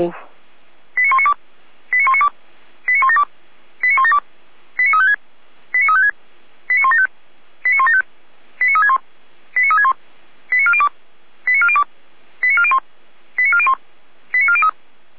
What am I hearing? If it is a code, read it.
Five-tone selective call (ZVEI), heard as quick runs of five tones, tones of about 70 ms, with speech. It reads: 89192 89192 89172 89172 89356 89356 89175 89175 89361 89361 89402 89402 89401 89401 89403